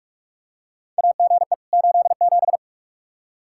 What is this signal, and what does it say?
Morse code, heard as a keyed tone, about 45 wpm, 700 Hz. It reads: AGE 87
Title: K1USN SST QSO Practice 45wpm